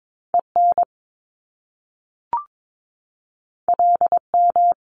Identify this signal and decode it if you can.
Morse code, heard as a keyed tone, about 22 wpm, 700 Hz. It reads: EN LM
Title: Top Two Letter Combinations in Words 22wpm